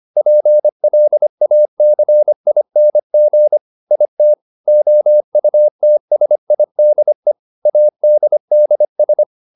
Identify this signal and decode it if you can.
Morse code, heard as a keyed tone, about 25 wpm, 600 Hz. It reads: PLACING IT OUTSIDE ADDS